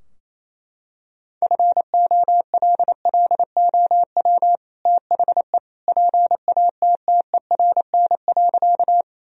Morse code, heard as a keyed tone, about 27 wpm, 700 Hz. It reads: FOLLOW THE PATTERN.
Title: Sentences from Top 1000 Words - Review Part 2 - 28wpm